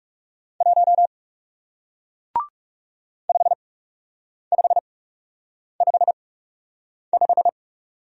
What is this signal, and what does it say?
Morse code, heard as a keyed tone, about 41 wpm, spaced as for 8 wpm, 700 Hz. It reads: 1 5555